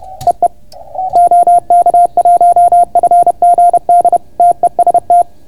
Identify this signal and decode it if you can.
Morse code, heard as a keyed tone, about 30 wpm, 690 Hz. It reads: EE OK1FGD TEST